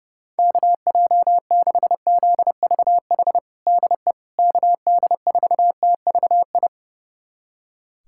Morse code, heard as a keyed tone, about 32 wpm, 700 Hz. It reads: KJ6ZVH DE KD4TVI
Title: USA Callsign DE Callsign 30wpm